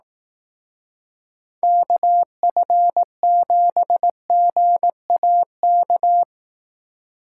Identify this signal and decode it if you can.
Morse code, heard as a keyed tone, about 18 wpm, 700 Hz. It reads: KF7GAK
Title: Speed Racing - USA 2x3 Call Signs 15wpm